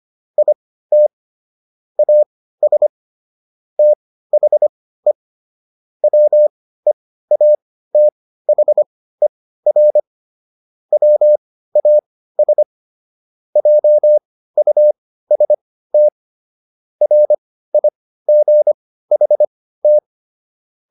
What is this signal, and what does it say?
Morse code, heard as a keyed tone, about 25 wpm, spaced as for 15 wpm, 600 Hz. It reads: IT AS THE WEATHER WAS JUST RIGHT